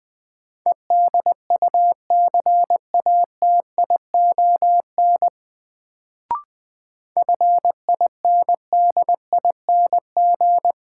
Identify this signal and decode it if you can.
Morse code, heard as a keyed tone, about 20 wpm, 700 Hz. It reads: EDUCATION FINDING
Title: Sentences from Top 1100 Words - New Word Drill Alt - 20wpm